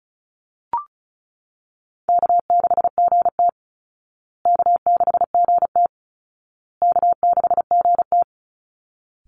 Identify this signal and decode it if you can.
Morse code, heard as a keyed tone, about 34 wpm, 700 Hz. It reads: K6GT K6GT K6GT
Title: Call Signs Repeated 2x - 35wpm